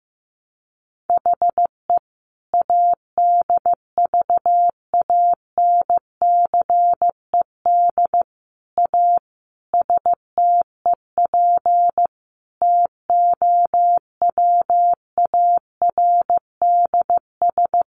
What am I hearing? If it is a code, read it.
Morse code, heard as a keyed tone, about 15 wpm, 700 Hz. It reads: HE ADVANCED A STEP TOWARDS
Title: Sentences from Top 1200 Words 15wpm